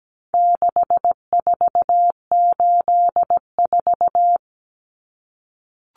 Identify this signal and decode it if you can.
Morse code, heard as a keyed tone, about 17 wpm, 700 Hz. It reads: 6484
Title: Numbers - 4-Digit 17wpm